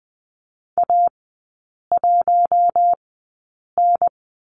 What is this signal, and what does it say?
Morse code, heard as a keyed tone, about 20 wpm, spaced as for 8 wpm, 700 Hz. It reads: A1N
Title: 133 Lesson 22e - Question Mark - Three Characters 20wpm